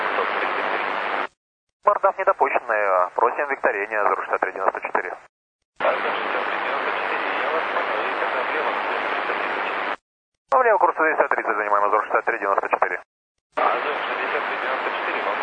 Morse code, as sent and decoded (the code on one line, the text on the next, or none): none